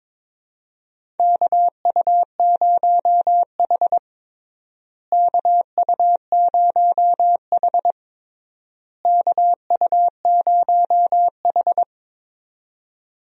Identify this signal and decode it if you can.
Morse code, heard as a keyed tone, about 22 wpm, 700 Hz. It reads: KU0H KU0H KU0H
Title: USA Call Signs - 4 Characters - Repeated 3x - 22wpm